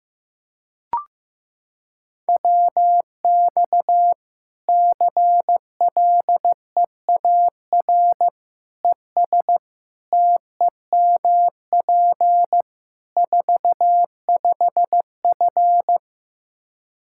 Morse code, heard as a keyed tone, about 16 wpm, 700 Hz. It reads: WX CLEAR ES TEMP 45F
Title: Practice QSO - Exchange 2 - Info - 15wpm